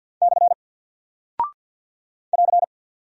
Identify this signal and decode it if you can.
Morse code, heard as a keyed tone, about 48 wpm, 700 Hz. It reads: / <AR>
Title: Single Character (Letter:Number:Punctuation:Prosign) 50wpm